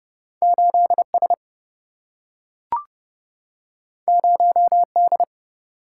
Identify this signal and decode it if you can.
Morse code, heard as a keyed tone, about 29 wpm, 700 Hz. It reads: 8S 0D
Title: Single-Letter Single-Number 30wpm